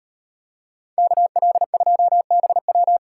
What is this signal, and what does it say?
Morse code, heard as a keyed tone, about 36 wpm, 700 Hz. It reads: KL2BW